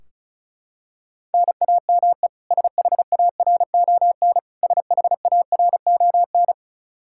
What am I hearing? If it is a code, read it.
Morse code, heard as a keyed tone, about 36 wpm, 700 Hz. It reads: NAME SHARON SHARON
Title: Practice QSO - Exchange 1 - Beginning - 35wpm